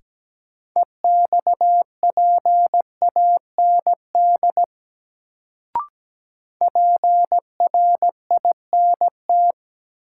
Morse code, heard as a keyed tone, about 17 wpm, 700 Hz. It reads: EXPAND PRINT
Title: Sentences from Top 1400 Words - New Word Drill - 17wpm